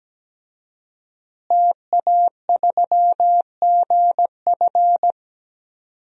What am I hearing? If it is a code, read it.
Morse code, heard as a keyed tone, about 17 wpm, 700 Hz. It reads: TA3GF
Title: DXCC-100 Call Signs - 5 Characters - Repeated 3x - 17wpm